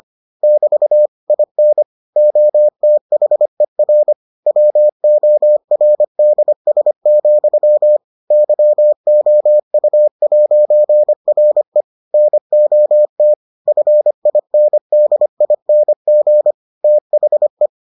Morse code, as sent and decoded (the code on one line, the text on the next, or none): -...- / .. -. / --- - .... .-. / .-- --- .-. -.. ... --..-- / -.-- --- ..- .----. .-. / -. --- - / ..-. .. -. -.. .. -. --. / - .... .
<BT> IN OTHER WORDS, YOU'RE NOT FINDING THE